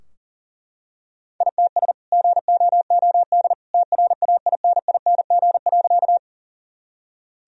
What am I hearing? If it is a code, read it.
Morse code, heard as a keyed tone, about 37 wpm, 700 Hz. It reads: ITS GOOD TRAINING.
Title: Sentences from Top 1000 Words - Review Part 4 - 40wpm